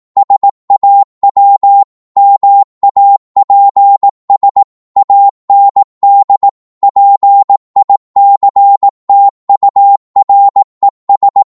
Morse code, heard as a keyed tone, about 18 wpm, 800 Hz. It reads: SAW MAPS AND PICTURES
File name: Morse / psg-18wpm